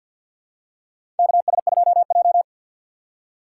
Morse code, heard as a keyed tone, about 47 wpm, 700 Hz. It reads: KS2J
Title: USA Call Signs - 4 Characters - 2x Character Spacing - 50wpm